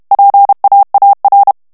Morse code, heard as a keyed tone, about 32 wpm, 800 Hz. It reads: PAAR
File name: HOLLANDAIS texte clair 2 vitesse 30 mots minute